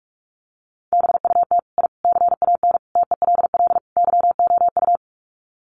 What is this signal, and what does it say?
Morse code, heard as a keyed tone, about 44 wpm, 700 Hz. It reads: BUT I CAN TELL YOU
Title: Sets of 5 Words Encore 45wpm